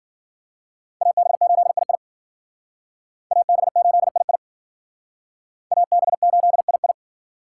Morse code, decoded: AB8II AB8II AB8II

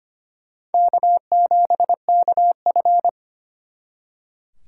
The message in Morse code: K7KF

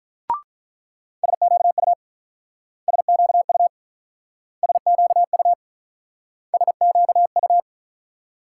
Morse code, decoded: SQU SQU SQU SQU

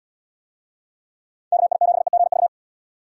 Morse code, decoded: KI7EGV

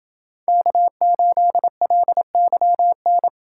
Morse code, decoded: K8LYN

